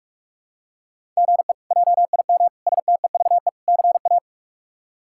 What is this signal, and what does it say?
Morse code, heard as a keyed tone, about 46 wpm, 700 Hz. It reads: GE JIM STEVE CA
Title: K1USN SST QSO Practice - 45wpm